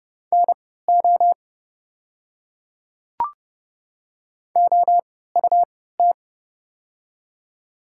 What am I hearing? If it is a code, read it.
Morse code, heard as a keyed tone, about 30 wpm, spaced as for 17 wpm, 700 Hz. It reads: NO OUT
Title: 3x Character Spacing - Top 200 Words - 30wpm